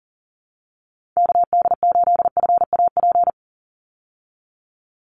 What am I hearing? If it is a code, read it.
Morse code, heard as a keyed tone, about 40 wpm, 700 Hz. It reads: KD8FAP
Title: USA Call Signs 40wpm